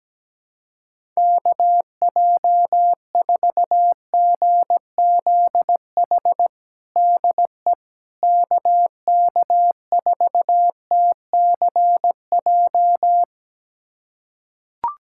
Morse code, heard as a keyed tone, about 17 wpm, 700 Hz. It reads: KJ4GZH DE KK4TCJ
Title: USA Callsign DE Callsign - 6 Characters - 17wpm